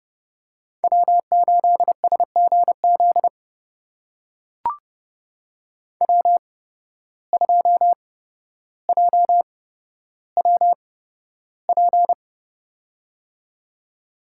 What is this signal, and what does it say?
Morse code, heard as a keyed tone, about 30 wpm, spaced as for 13 wpm, 700 Hz. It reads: W8SGZ W2JWP
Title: USA Call Signs - 5 Characters - 8x Character Spacing - 30wpm